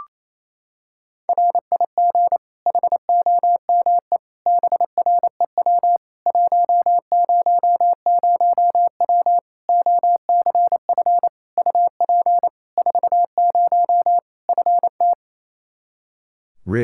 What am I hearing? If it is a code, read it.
Morse code, heard as a keyed tone, about 28 wpm, 700 Hz. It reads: RIG HOME BREW 100W OCF UP 40 FT
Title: K1USN SST QSO Practice 28wpm